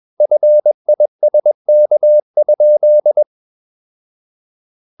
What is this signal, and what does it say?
Morse code, heard as a keyed tone, about 21 wpm, 600 Hz. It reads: FISK?
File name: Ferie QRS 21wpm